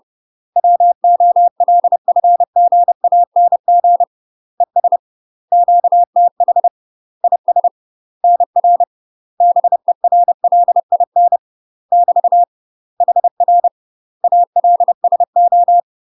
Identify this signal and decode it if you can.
Morse code, heard as a keyed tone, about 30 wpm, 700 Hz. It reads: WOLFGANG ES QTH IS NR BERLIN <BT> HR ALSO
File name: UK MASTER QSOs TRAINING / 30WPM / UK MASTER QSOs 0002